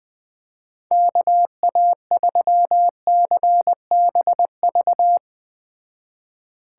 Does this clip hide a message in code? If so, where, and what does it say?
Yes: Morse code from 0.9 to 5.2 s, heard as a keyed tone, about 19 wpm, 700 Hz. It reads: KA3CBV